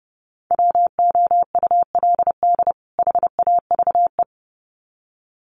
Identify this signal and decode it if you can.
Morse code, heard as a keyed tone, about 30 wpm, 700 Hz. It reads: WOULD HAVE
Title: Sets of 2 Words 30wpm